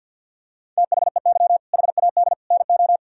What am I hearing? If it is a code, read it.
Morse code, heard as a keyed tone, about 50 wpm, 700 Hz. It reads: THEY HAD NO
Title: Sentences from Top 1600 Words 50wpm